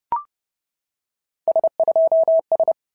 Morse code, heard as a keyed tone, about 31 wpm, 650 Hz. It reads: S2S
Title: SOTA QSO Practice 30wpm